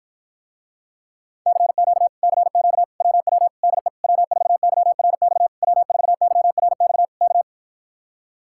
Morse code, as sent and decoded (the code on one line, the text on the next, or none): -.-. --.- / -.-. --.- / .-- .-- / -.. / .-- ....- -.-- .-. -..- / .-- ....- -.-- .-. -..- / -.-
CQ CQ WW DE W4YRX W4YRX K